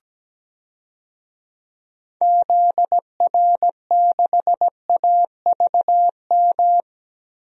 Morse code, heard as a keyed tone, about 17 wpm, 700 Hz. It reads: ZR6AVM
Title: DXCC-100 Call Signs - 6 Characters - Repeated 3x - 17wpm